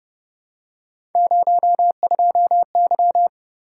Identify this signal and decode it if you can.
Morse code, heard as a keyed tone, about 31 wpm, 700 Hz. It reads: 02Y
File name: Speed-Racing - Three Letter-Number Combinations - 25wpm